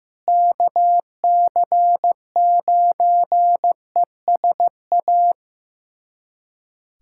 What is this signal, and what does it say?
Morse code, heard as a keyed tone, about 15 wpm, 700 Hz. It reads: KC9ESA
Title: USA Callsign DE Callsign - 6 Characters - 15wpm